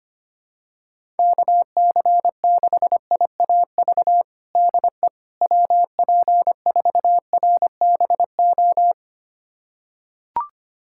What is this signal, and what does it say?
Morse code, heard as a keyed tone, about 25 wpm, 700 Hz. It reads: KC6IAV DE WP4RBO
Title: USA Callsign DE Callsign 25wpm